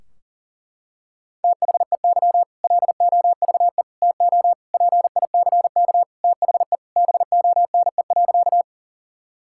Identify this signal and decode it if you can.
Morse code, heard as a keyed tone, about 37 wpm, 700 Hz. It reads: THEY LOVE TO PICK THE BONE.
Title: Sentences from Top 1600 Words 40wpm